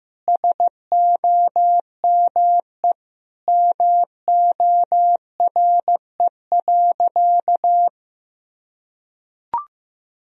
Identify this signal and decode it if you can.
Morse code, heard as a keyed tone, about 16 wpm, 700 Hz. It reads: SOME MORE.
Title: Sentences from Top 1000 Words - Review Part 2 - 15wpm